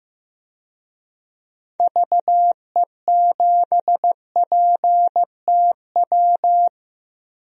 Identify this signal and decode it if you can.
Morse code, heard as a keyed tone, about 15 wpm, 700 Hz. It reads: VE7PTW
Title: DXCC-100 Call Signs - 6 Characters - 15wpm